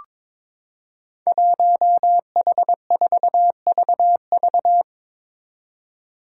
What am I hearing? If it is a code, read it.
Morse code, heard as a keyed tone, about 22 wpm, 700 Hz. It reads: 1H4VV
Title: Five Challenging Letter-Number Combinations 22wpm